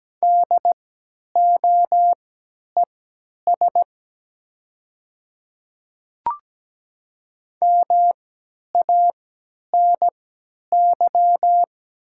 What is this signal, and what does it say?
Morse code, heard as a keyed tone, about 17 wpm, spaced as for 10 wpm, 700 Hz. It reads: DOES MANY
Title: 3x Character Spacing - Top 200 Words - 17wpm